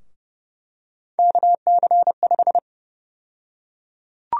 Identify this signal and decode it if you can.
Morse code, heard as a keyed tone, about 30 wpm, 700 Hz. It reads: KC5
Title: USA Call Sign Prefixes - 30wpm